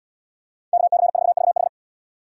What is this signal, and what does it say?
Morse code, heard as a keyed tone, about 69 wpm, 700 Hz. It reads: KC4VH